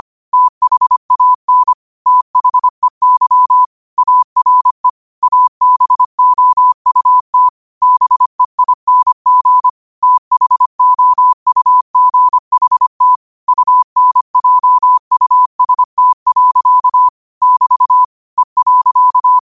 Morse code, heard as a keyed tone, about 25 wpm, 1000 Hz. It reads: THAN THEY ARE ABOUT BEING THOUGHT UNJUST. <BT> E.
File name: qotdfiles / 25wpm